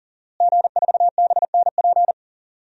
Morse code, heard as a keyed tone, about 40 wpm, 700 Hz. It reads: G4BNP
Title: DXCC-100 Call Signs - 5 Characters - Repeated 3x - 40wpm